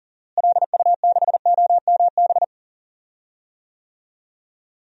Morse code, heard as a keyed tone, about 40 wpm, 700 Hz. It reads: LU6OMB
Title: DXCC-100 Call Signs - 6 Characters - Repeated 3x - 40wpm